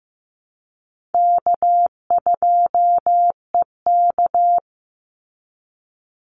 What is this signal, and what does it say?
Morse code, heard as a keyed tone, about 15 wpm, 700 Hz. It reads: K2EK
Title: Easy Call Signs Repeated 2x - 15wpm